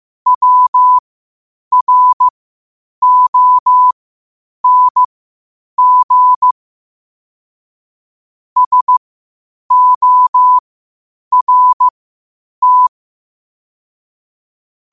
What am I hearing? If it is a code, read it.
Morse code, heard as a keyed tone, about 15 wpm, spaced as for 8 wpm, 1000 Hz. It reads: WRONG SORT
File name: qotdfiles / 8wpm